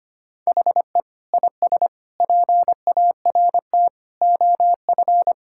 Morse code, heard as a keyed tone, about 25 wpm, 700 Hz. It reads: HE IS PART OF
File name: practice / 25wpm / Sentences with 3-Word Phrases - 25wpm